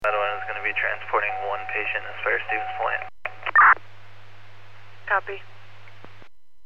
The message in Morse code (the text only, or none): none